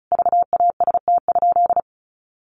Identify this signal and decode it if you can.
Morse code, heard as a keyed tone, about 34 wpm, 700 Hz. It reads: VAST?
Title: Sentences from Top 1300 Words 35wpm